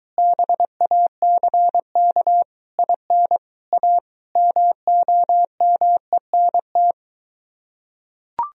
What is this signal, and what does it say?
Morse code, heard as a keyed tone, about 23 wpm, 700 Hz. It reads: BACK IN A MOMENT